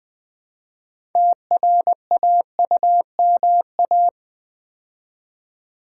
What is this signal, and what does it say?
Morse code, heard as a keyed tone, about 20 wpm, 700 Hz. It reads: TRAUMA
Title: Build A Word - 3 Letters At A Time - 6-Letter Words - 20wpm